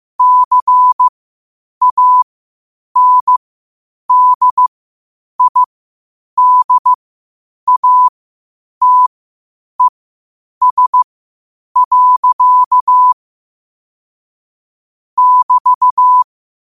Morse code, heard as a keyed tone, about 15 wpm, spaced as for 8 wpm, 1000 Hz. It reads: CANDIDATES. <BT>
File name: qotdfiles / 8wpm